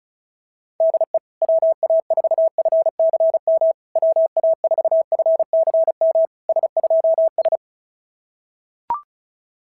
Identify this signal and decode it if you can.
Morse code, heard as a keyed tone, about 33 wpm, 660 Hz. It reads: DE WA4FCM WA4FCM S2S
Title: SOTA QSO Practice 35wpm